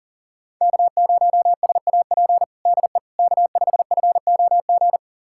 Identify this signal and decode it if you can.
Morse code, heard as a keyed tone, about 40 wpm, 700 Hz. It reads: K0SAP DE K5FOG
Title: USA Callsign DE Callsign - 5 Characters - 40wpm